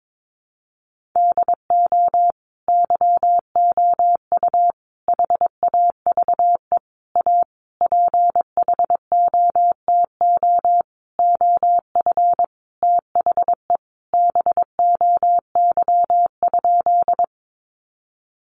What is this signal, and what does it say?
Morse code, heard as a keyed tone, about 22 wpm, 700 Hz. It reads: DO YOU HAVE A PHOTO OF THE BOY?
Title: Sentences from Top 1200 Words 22wpm